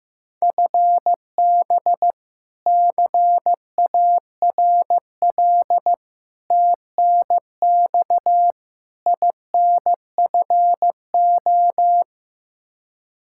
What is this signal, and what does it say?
Morse code, heard as a keyed tone, about 15 wpm, 700 Hz. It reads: FB CARL TNX INFO
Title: Practice QSO - Exchange 3 - More Info - 15wpm